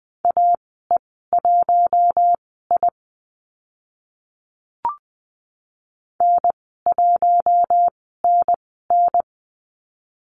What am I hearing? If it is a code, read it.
Morse code, heard as a keyed tone, about 20 wpm, spaced as for 15 wpm, 700 Hz. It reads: AE1I N1NN
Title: 042 Lesson 8g - Number 1 - Call Signs 20wpm